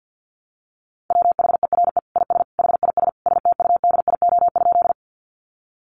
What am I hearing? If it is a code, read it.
Morse code, heard as a keyed tone, about 50 wpm, 700 Hz. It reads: WHERE IS HIS STUDIO?
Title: Sentences from Top 1200 Words 50wpm